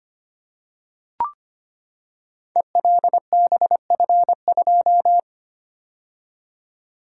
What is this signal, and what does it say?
Morse code, heard as a keyed tone, about 25 wpm, 700 Hz. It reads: ELBF2
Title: Code Groups - 5 Letter:Numbers - 25wpm